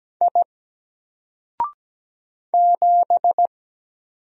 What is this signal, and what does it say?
Morse code, heard as a keyed tone, about 17 wpm, 700 Hz. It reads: I 7
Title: Single Character (Letter:Number:Punctuation:Prosign) Alt 17wpm